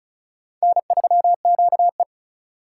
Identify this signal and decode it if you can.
Morse code, heard as a keyed tone, about 34 wpm, 700 Hz. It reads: N3QE
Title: K1USN SST QSO Practice - 35wpm